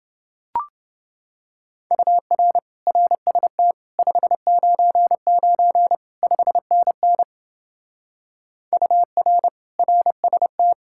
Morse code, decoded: UR RST 599 5NN UR RST

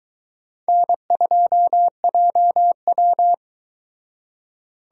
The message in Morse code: N2JW

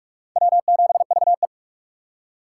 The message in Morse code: W7VE

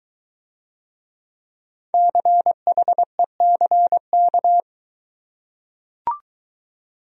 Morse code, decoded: CHECK

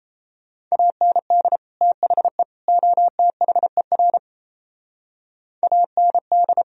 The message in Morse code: AND THE OTHER AND

